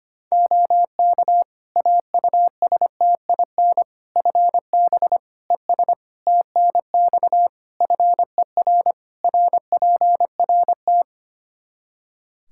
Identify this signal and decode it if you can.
Morse code, heard as a keyed tone, about 25 wpm, 700 Hz. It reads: OK AUSTIN FB ES TNX FER RPRT
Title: Practice QSO - Exchange 2 - Info - 25wpm